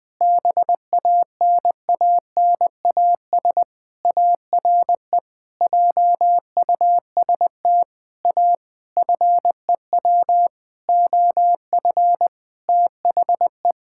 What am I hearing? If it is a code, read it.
Morse code, heard as a keyed tone, about 21 wpm, 700 Hz. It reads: BANANAS ARE JUST A FEW OF THE
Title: Sentences with 6-Word Phrases - 20wpm